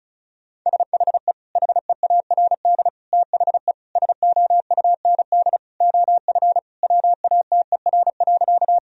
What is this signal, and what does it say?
Morse code, heard as a keyed tone, about 36 wpm, 700 Hz. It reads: SHE HEARD THE SOUND OF WATER.
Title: Sentences from Top 1000 Words - Review Part 1 - 35wpm